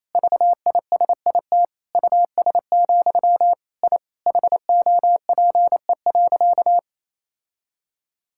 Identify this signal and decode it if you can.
Morse code, heard as a keyed tone, about 27 wpm, 700 Hz. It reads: VISIT US, I HOPE.
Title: Sentences from Top 1000 Words - Review Part 6 - 28wpm